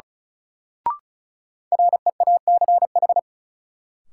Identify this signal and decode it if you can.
Morse code, heard as a keyed tone, about 34 wpm, 700 Hz. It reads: REACH
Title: Sentences from Top 400 Words - New Word Drill - 35wpm